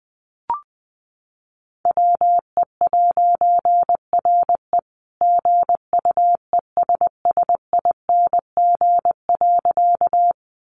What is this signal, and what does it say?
Morse code, heard as a keyed tone, about 20 wpm, 700 Hz. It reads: WE'RE GUESSING.
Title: A Sound Of Thunder 20wpm